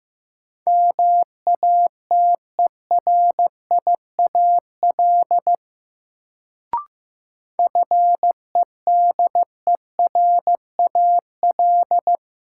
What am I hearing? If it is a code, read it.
Morse code, heard as a keyed tone, about 15 wpm, 700 Hz. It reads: MATERIAL FEDERAL
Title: Sentences from Top 1000 Words - Review Part 5 - New Word Drill - 15wpm